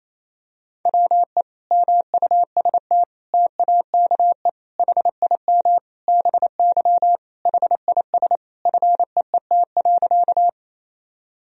Morse code, heard as a keyed tone, about 28 wpm, 700 Hz. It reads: WE MUST TAKE HIM BY HIS FEET.
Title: Sentences from Top 1000 Words - Review Part 1 - 28wpm